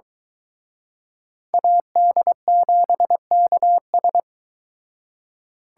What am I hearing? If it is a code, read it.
Morse code, heard as a keyed tone, about 24 wpm, 700 Hz. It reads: AD7KS